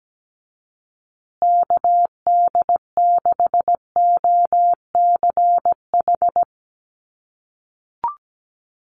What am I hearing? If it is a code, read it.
Morse code, heard as a keyed tone, about 17 wpm, 700 Hz. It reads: KD6OCH
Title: USA Call Signs Repeated 3x 17wpm